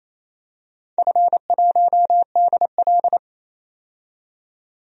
Morse code, heard as a keyed tone, about 28 wpm, 700 Hz. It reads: F1DL
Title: DXCC-100 Call Signs - 4 Characters - 28wpm